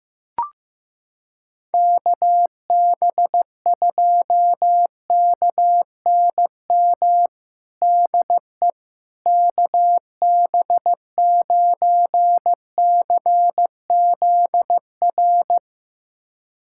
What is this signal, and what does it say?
Morse code, heard as a keyed tone, about 15 wpm, 700 Hz. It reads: KB2KNM DE KB9CZR
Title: USA Callsign DE Callsign - 6 Characters - 15wpm